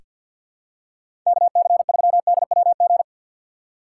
Morse code, heard as a keyed tone, about 48 wpm, 700 Hz. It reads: KC3DWG
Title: USA Call Signs - 6 Characters - 6x Character Spacing - 50wpm